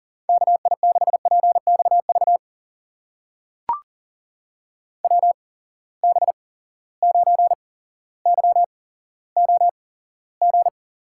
Morse code, heard as a keyed tone, about 40 wpm, spaced as for 17 wpm, 700 Hz. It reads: KI6PXV WB9YOG
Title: USA Call Signs - 6 Characters - 8x Character Spacing - 40wpm